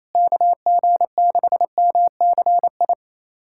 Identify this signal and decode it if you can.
Morse code, heard as a keyed tone, about 28 wpm, 700 Hz. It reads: KG6MCI
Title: USA Callsign DE Callsign 28wpm